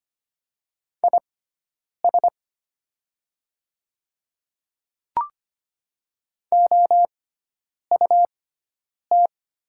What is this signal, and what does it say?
Morse code, heard as a keyed tone, about 25 wpm, spaced as for 9 wpm, 700 Hz. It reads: IS OUT
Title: 6x Character Spacing - Top 100 Words - 25wpm